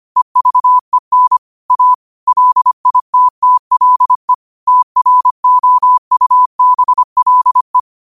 Morse code, heard as a keyed tone, about 25 wpm, 1000 Hz. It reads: EVEN A LITTLE TROUBLE
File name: qotdfiles / 25wpm